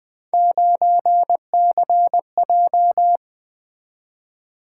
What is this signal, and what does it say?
Morse code, heard as a keyed tone, about 20 wpm, 700 Hz. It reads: 9CJ
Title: Three Letter-Number Combinations - 20wpm